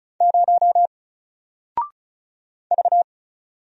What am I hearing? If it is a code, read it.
Morse code, heard as a keyed tone, about 36 wpm, 700 Hz. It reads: 0 V
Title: Single Character (Letter:Number:Punctuation:Prosign) Alt 35wpm